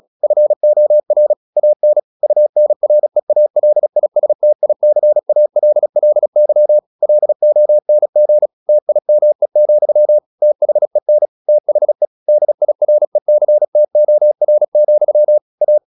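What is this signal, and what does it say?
Morse code, heard as a keyed tone, about 36 wpm, 600 Hz. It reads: FOR AN UNREALISTICALLY LONG TIME, THEN THE DIRECTOR, A
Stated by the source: Morse practice at 36 wpm, text of Ken Follett's novel Never, from CW med Gnister.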